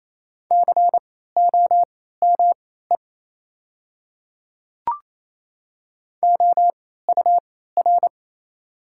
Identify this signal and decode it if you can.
Morse code, heard as a keyed tone, about 28 wpm, spaced as for 16 wpm, 690 Hz. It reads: COME OUR
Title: 3x Character Spacing - Top 500 Words - 28wpm